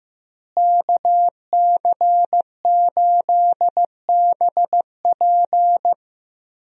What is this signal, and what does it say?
Morse code, heard as a keyed tone, about 16 wpm, 700 Hz. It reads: KC8BP
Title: USA Call Signs - 5 Characters - Repeated 2x - 15wpm